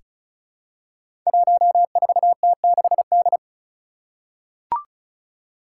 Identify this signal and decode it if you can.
Morse code, heard as a keyed tone, about 36 wpm, 700 Hz. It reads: 14T6D
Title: Five Challenging Letter-Number Combinations 35wpm